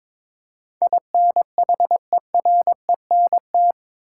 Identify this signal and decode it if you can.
Morse code, heard as a keyed tone, about 22 wpm, 700 Hz. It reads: INHERENT
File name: practice / 22wpm / Eight Letter Words - 22wpm